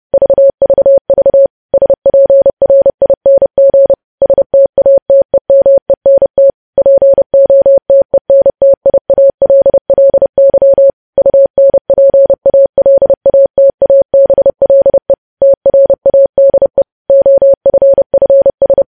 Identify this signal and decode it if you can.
Morse code, heard as a keyed tone, about 30 wpm, 550 Hz. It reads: VVV SPRING STATEMENT POTENTIALLY UNPALATABLE TRADE OFFS
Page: news headlines in morse code at 30 wpm